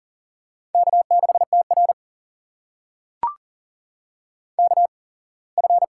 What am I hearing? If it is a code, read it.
Morse code, heard as a keyed tone, about 40 wpm, spaced as for 24 wpm, 700 Hz. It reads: K6TR KF
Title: USA Call Signs - 8x Character Spacing - 40wpm